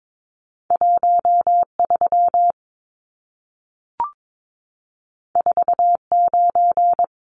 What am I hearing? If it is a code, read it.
Morse code, heard as a keyed tone, about 22 wpm, 700 Hz. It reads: 13 49
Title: Numbers - 2-Digit 22wpm